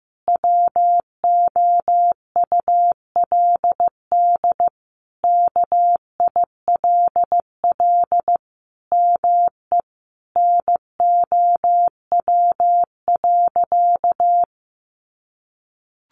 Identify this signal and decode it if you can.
Morse code, heard as a keyed tone, about 15 wpm, 700 Hz. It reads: WOULD KILL ME NOW.
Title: Sentences from Top 1400 Words 15wpm